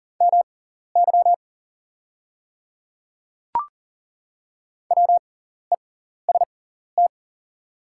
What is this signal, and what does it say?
Morse code, heard as a keyed tone, about 38 wpm, spaced as for 14 wpm, 700 Hz. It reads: MY WEST
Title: 6x Character Spacing - Top 500 Words - 40wpm